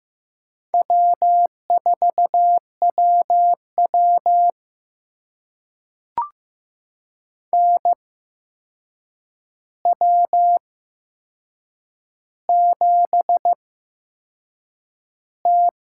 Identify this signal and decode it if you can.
Morse code, heard as a keyed tone, about 15 wpm, spaced as for 6 wpm, 700 Hz. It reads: W4WW NW7T
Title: USA Call Signs - 4 Characters - 8x Character Spacing - 15wpm